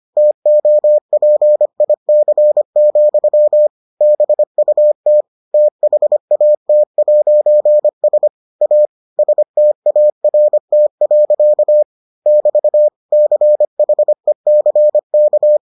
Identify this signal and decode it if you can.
Morse code, heard as a keyed tone, about 25 wpm, 600 Hz. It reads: TOPIC, BUT THAT'S A START. <BT> CHECK